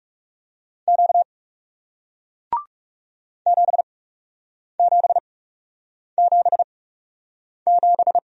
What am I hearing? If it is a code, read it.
Morse code, heard as a keyed tone, about 39 wpm, spaced as for 8 wpm, 700 Hz. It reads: Q 7777